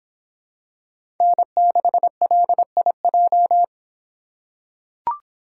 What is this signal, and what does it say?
Morse code, heard as a keyed tone, about 25 wpm, 700 Hz. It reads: N6LIJ